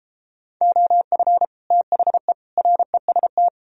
Morse code, heard as a keyed tone, about 33 wpm, 700 Hz. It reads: OF THE REST